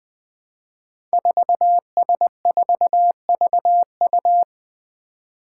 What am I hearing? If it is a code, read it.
Morse code, heard as a keyed tone, about 21 wpm, 700 Hz. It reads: 4S4VU
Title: Five Challenging Letter-Number Combinations 20wpm